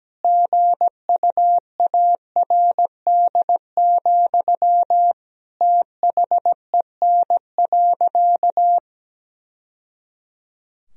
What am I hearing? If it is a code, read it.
Morse code, heard as a keyed tone, about 17 wpm, 700 Hz. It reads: GUARD, THEN.
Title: Sentences from Top 1000 Words - Review Part 4 - 17wpm